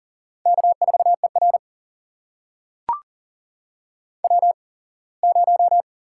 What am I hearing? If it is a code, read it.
Morse code, heard as a keyed tone, about 40 wpm, spaced as for 24 wpm, 700 Hz. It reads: K4ER W0
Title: USA Call Signs - 8x Character Spacing - 40wpm